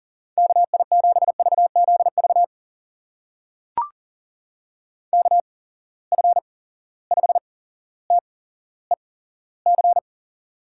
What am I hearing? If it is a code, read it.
Morse code, heard as a keyed tone, about 39 wpm, spaced as for 17 wpm, 700 Hz. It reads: KI7VZV KF5TEC
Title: USA Call Signs - 6 Characters - 8x Character Spacing - 40wpm